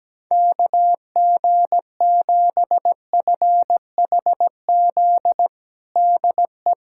Morse code, heard as a keyed tone, about 17 wpm, 700 Hz. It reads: KG7FHZ DE
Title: USA Callsign DE Callsign - 6 Characters - 17wpm